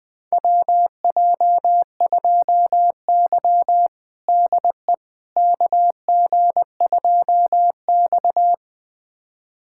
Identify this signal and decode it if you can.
Morse code, heard as a keyed tone, about 21 wpm, 700 Hz. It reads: WJ2Y DE KG2X
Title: USA Callsign DE Callsign - 4 Characters - 20wpm